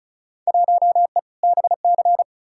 Morse code, heard as a keyed tone, about 36 wpm, 700 Hz. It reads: 1E BC
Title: Field Day Exchanges - 35wpm